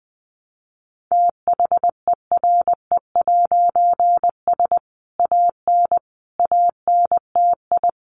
Morse code, heard as a keyed tone, about 20 wpm, 700 Hz. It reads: THERE'S AN ANTI
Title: A Sound Of Thunder 20wpm